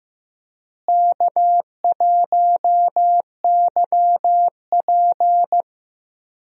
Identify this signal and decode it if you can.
Morse code, heard as a keyed tone, about 15 wpm, 700 Hz. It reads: K1YP
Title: USA Call Signs - 4 Characters - Repeated 3x - 15wpm